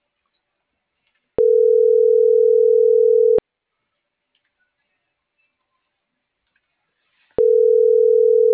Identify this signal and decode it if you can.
Ringback tone (North America), 440 + 480 Hz.